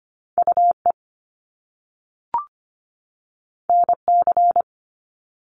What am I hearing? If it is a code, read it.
Morse code, heard as a keyed tone, about 25 wpm, 700 Hz. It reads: UE NC
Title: Top Two Letter Combinations in Words 25wpm